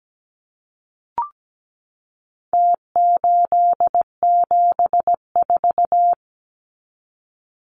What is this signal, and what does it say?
Morse code, heard as a keyed tone, about 17 wpm, 700 Hz. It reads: T874
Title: Numbers - Contest Style - 4-Digit 17wpm